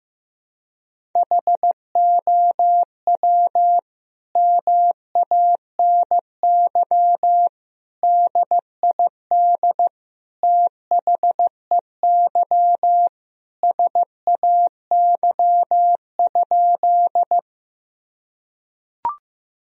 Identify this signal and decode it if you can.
Morse code, heard as a keyed tone, about 15 wpm, 700 Hz. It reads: HOW MANY DID THEY SAY?
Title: Sentences from Top 1000 Words - Review Part 1 - 15wpm